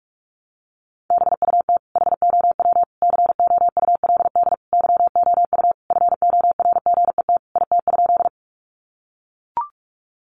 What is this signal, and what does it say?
Morse code, heard as a keyed tone, about 44 wpm, 700 Hz. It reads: BUT HOW COULD YOU FORGET IT?
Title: Sentences from Top 1200 Words 45wpm